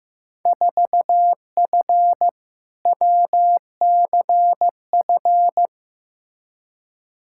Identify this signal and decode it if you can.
Morse code, heard as a keyed tone, about 15 wpm, 700 Hz. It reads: 4F WCF
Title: Field Day Exchanges - 15wpm